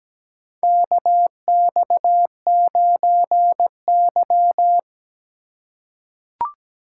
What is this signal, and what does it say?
Morse code, heard as a keyed tone, about 17 wpm, 700 Hz. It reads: KX9Y
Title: USA Call Signs - 4 Characters - 4x Character Spacing - 17wpm